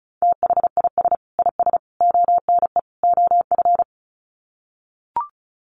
Morse code, heard as a keyed tone, about 34 wpm, 700 Hz. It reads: THIS IS ONE OF
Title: Sets of 4 Words 35wpm